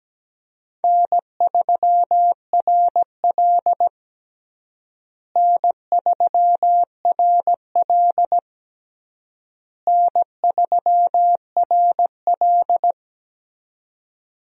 Morse code, heard as a keyed tone, about 17 wpm, 700 Hz. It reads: N3RL N3RL N3RL
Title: USA Call Signs - 4 Characters - Repeated 3x - 17wpm